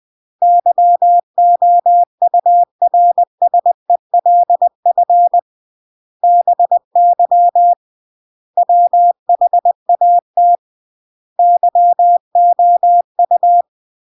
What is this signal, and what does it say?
Morse code, heard as a keyed tone, about 20 wpm, 700 Hz. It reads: YOURSELF BY WHAT YOU
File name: CW / jodorowsky / 82 commandments 20-20